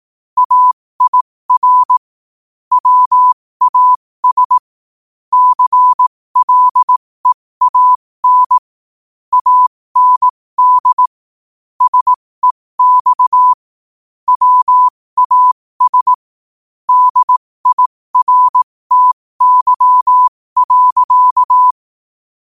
Morse code, heard as a keyed tone, about 18 wpm, spaced as for 15 wpm, 1000 Hz. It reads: AIR WAS CLEAN AND SEX WAS DIRTY.
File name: qotdfiles / 15wpm